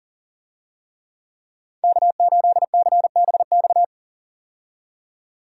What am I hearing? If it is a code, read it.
Morse code, heard as a keyed tone, about 40 wpm, 700 Hz. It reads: K8CBX
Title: USA Call Signs - 5 Characters - 40wpm